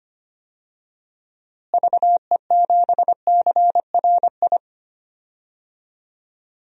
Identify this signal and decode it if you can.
Morse code, heard as a keyed tone, about 25 wpm, 700 Hz. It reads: VE7CRI
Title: DXCC-100 Call Signs - 6 Characters - Repeated 3x - 25wpm